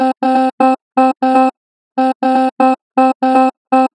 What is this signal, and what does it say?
There is no code.